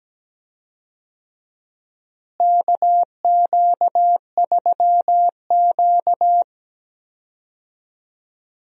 Morse code, heard as a keyed tone, about 17 wpm, 700 Hz. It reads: KQ3Q
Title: USA Call Signs - 4 Characters - Repeated 3x - 17wpm